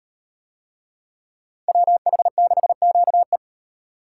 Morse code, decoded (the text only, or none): WH6QE